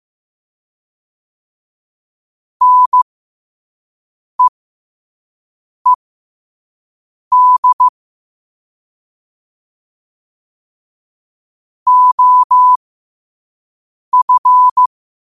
Morse code, heard as a keyed tone, about 15 wpm, spaced as for 5 wpm, 1000 Hz. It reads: NEED OF
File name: qotdfiles / 5wpm